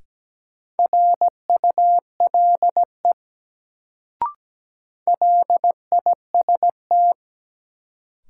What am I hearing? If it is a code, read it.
Morse code, heard as a keyed tone, about 17 wpm, 700 Hz. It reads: RULE LIST
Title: Sentences from Top 400 Words - New Word Drill - 17wpm